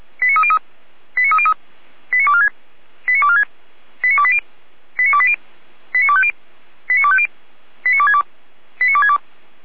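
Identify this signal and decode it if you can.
Five-tone selective call (ZVEI), heard as quick runs of five tones, tones of about 70 ms. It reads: 89303 89303 89246 89246 89280 89280 89250 89250 89262 89262
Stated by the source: Five-tone ZVEI selective call, received by SDR, fire brigade Orlamünde.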